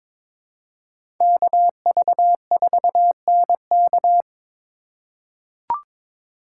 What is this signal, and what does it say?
Morse code, heard as a keyed tone, about 22 wpm, 700 Hz. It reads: KV4NK
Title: USA Call Signs - 2x Character Spacing - 22wpm